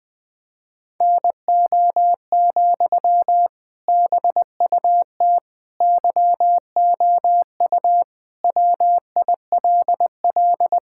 Morse code, heard as a keyed tone, about 21 wpm, 700 Hz. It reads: NO, BUT YOU WILL
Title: Sentences from Top 1000 Words - Review Part 6 - 20wpm